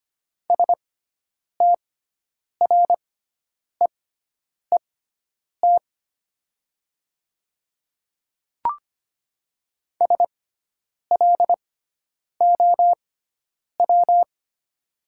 Morse code, heard as a keyed tone, about 25 wpm, spaced as for 9 wpm, 700 Hz. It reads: STREET SLOW
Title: 6x Character Spacing - Top 500 Words - 25wpm